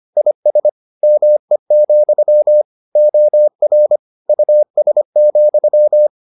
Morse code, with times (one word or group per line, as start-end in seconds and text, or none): 0.2-0.7 IS
1.0-2.6 ME,
3.0-4.0 OR
4.3-6.1 US,